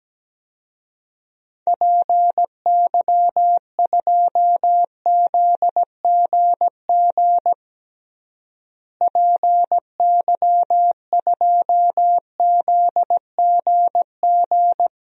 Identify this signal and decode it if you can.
Morse code, heard as a keyed tone, about 17 wpm, 700 Hz. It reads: PY2ZGG PY2ZGG
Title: DXCC-100 Call Signs - 6 Characters - Repeated 2x - 17wpm